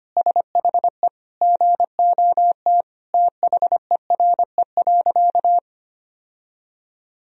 Morse code, heard as a keyed tone, about 25 wpm, 700 Hz. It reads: SHE GOT THERE.
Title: Sentences from Top 1000 Words - Review Part 1 - 25wpm